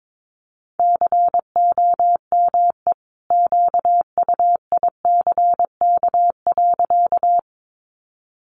Morse code, heard as a keyed tone, about 22 wpm, 700 Hz. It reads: COME QUICK.
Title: Sentences from Top 500 Words 22wpm